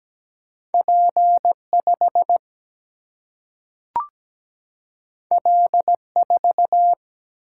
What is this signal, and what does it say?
Morse code, heard as a keyed tone, about 17 wpm, 700 Hz. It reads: P 5 L 4
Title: Single-Letter Single-Number 17wpm